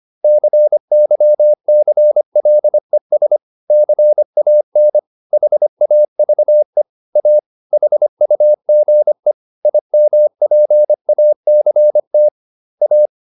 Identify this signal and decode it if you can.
Morse code, heard as a keyed tone, about 25 wpm, 600 Hz. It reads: CYCLES CAN HAVE A HUGE IMPACT A